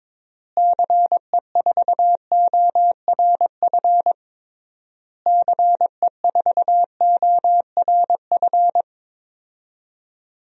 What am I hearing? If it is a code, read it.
Morse code, heard as a keyed tone, about 21 wpm, 700 Hz. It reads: CE4ORF CE4ORF
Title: DXCC-100 Call Signs - 6 Characters - Repeated 3x - 22wpm